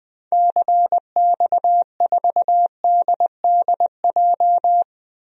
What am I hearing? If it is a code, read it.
Morse code, heard as a keyed tone, about 21 wpm, 700 Hz. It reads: CX4DDJ
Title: DXCC-100 Call Signs - 6 Characters - 20wpm